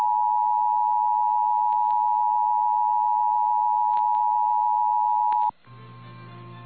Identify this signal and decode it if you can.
Emergency Alert System (SAME) tones (the 853 and 960 Hz attention tone). It is heard from 0.2 to 6.4 s.
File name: eas / KNCYKBIE / KNCYKBIE